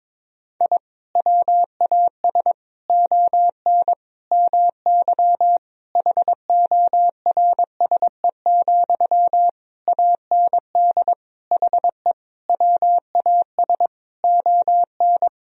Morse code, heard as a keyed tone, about 22 wpm, 700 Hz. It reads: I WAS ON MY HORSE, AND HE WAS ON
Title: Sentences from Top 1000 Words - Review Part 3 - 22wpm